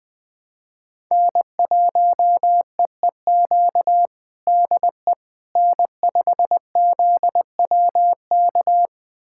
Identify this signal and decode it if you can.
Morse code, heard as a keyed tone, about 20 wpm, 700 Hz. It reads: N1EEQ DE N5ZWK
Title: USA Callsign DE Callsign - 5 Characters - 20wpm